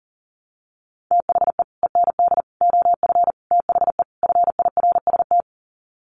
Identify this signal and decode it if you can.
Morse code, heard as a keyed tone, about 40 wpm, 700 Hz. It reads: THE END OF THE FIRST
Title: Sets of 5 Words Encore 40wpm